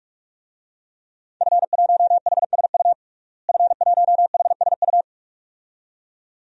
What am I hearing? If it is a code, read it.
Morse code, heard as a keyed tone, about 46 wpm, 700 Hz. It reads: F1HSU F1HSU
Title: DXCC-100 Call Signs - 5 Characters - Repeated 2x - 45wpm